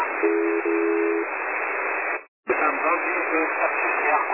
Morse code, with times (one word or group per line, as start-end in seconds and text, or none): none